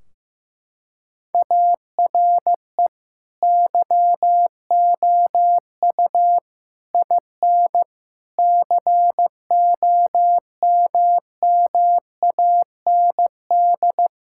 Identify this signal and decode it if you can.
Morse code, heard as a keyed tone, about 15 wpm, 700 Hz. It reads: ARE YOU IN COMMAND
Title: Sentences from Top 1600 Words 15wpm